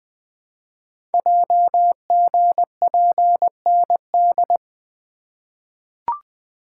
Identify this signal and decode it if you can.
Morse code, heard as a keyed tone, about 20 wpm, 700 Hz. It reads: JGPND
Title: Code Groups - 5 Letters - 20wpm